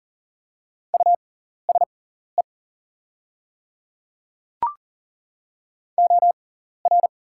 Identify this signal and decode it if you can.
Morse code, heard as a keyed tone, about 40 wpm, spaced as for 14 wpm, 700 Hz. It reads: USE OR
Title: 6x Character Spacing - Top 100 Words - 40wpm